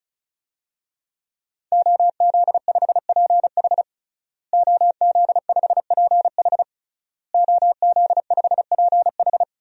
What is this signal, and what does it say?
Morse code, heard as a keyed tone, about 36 wpm, 700 Hz. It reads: OZ5PH OZ5PH OZ5PH
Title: DXCC-100 Call Signs - 5 Characters - Repeated 3x - 35wpm